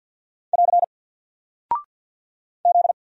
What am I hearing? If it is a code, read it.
Morse code, heard as a keyed tone, about 48 wpm, 700 Hz. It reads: <AR> Z
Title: Single Character (Letter:Number:Punctuation:Prosign) Alt 50wpm